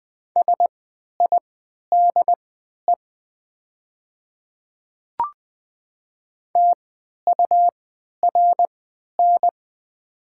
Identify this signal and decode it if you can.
Morse code, heard as a keyed tone, about 20 wpm, spaced as for 11 wpm, 700 Hz. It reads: SIDE TURN
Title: 3x Character Spacing - Top 200 Words - 20wpm